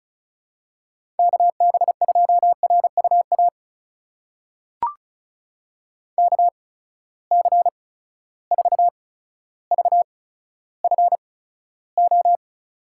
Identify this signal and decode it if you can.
Morse code, heard as a keyed tone, about 36 wpm, spaced as for 15 wpm, 700 Hz. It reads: KB2RUA KC4VFO
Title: USA Call Signs - 8x Character Spacing - 35wpm